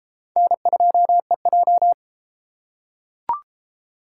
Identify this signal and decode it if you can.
Morse code, heard as a keyed tone, about 31 wpm, 700 Hz. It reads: N2EJ